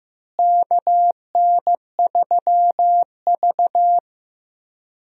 Morse code, heard as a keyed tone, about 15 wpm, 700 Hz. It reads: KN3V
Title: USA Call Signs - 4 Characters - Repeated 3x - 15wpm